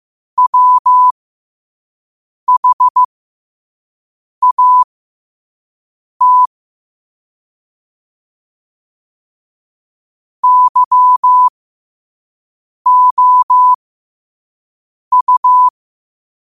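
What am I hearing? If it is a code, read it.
Morse code, heard as a keyed tone, about 15 wpm, spaced as for 5 wpm, 1000 Hz. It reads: WHAT YOU
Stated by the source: Morse code sent at 5 wpm for learning.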